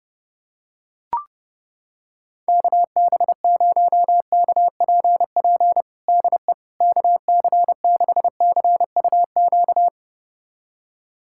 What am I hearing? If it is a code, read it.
Morse code, heard as a keyed tone, about 30 wpm, 700 Hz. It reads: KB0KPP DE KC6CUQ
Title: USA Callsign DE Callsign 30wpm